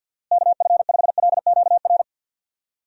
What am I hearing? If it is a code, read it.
Morse code, heard as a keyed tone, about 52 wpm, 700 Hz. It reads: KF5LQR
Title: USA Call Signs - 6 Characters - Repeated 3x - 50wpm